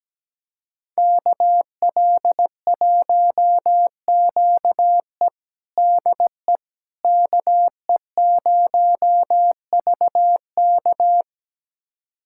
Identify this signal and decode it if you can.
Morse code, heard as a keyed tone, about 17 wpm, 700 Hz. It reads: KL1QE DE KE0VK
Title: USA Callsign DE Callsign - 5 Characters - 17wpm